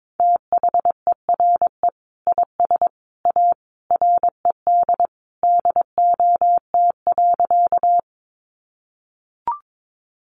Morse code, heard as a keyed tone, about 22 wpm, 700 Hz. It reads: THERE IS A RED DOT.
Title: Sentences from Top 1500 Words 22wpm